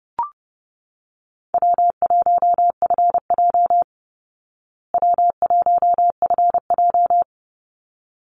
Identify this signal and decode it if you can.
Morse code, heard as a keyed tone, about 30 wpm, 700 Hz. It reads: W1FJ W1FJ
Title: Easy Call Signs Repeated 2x - 30wpm